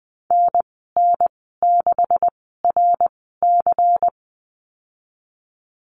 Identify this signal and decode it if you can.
Morse code, heard as a keyed tone, about 20 wpm, 700 Hz. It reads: N N 6 R C
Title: 154 Lesson 25g - Letter p - Call Signs 20wpm